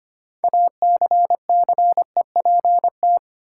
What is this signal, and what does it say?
Morse code, heard as a keyed tone, about 25 wpm, 700 Hz. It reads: ACCEPT
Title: Sentences from Top 1300 Words - New Word Drill Alt - 25wpm